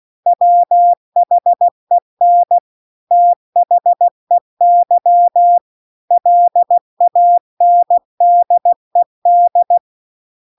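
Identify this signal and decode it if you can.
Morse code, heard as a keyed tone, about 16 wpm, 700 Hz. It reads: WHEN THEY LANDED
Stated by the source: Morse practice text at 16 WPM.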